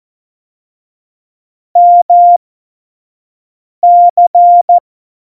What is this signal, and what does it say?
Morse code, heard as a keyed tone, about 14 wpm, spaced as for 5 wpm, 700 Hz. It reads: MC
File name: Morsekurs / Witze / Witz 03. November 2020, 25 F70